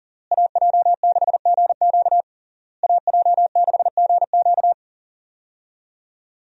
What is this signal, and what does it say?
Morse code, heard as a keyed tone, about 40 wpm, 700 Hz. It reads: AJ6GQ AJ6GQ
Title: USA Call Signs - 5 Characters - Repeated 3x - 40wpm